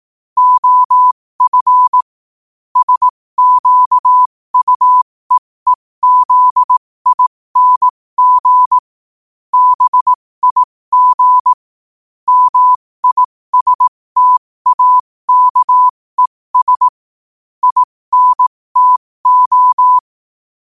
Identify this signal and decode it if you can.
Morse code, heard as a keyed tone, about 18 wpm, spaced as for 15 wpm, 1000 Hz. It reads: OF SQUEEZING BIG MISTAKES INTO
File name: qotdfiles / 15wpm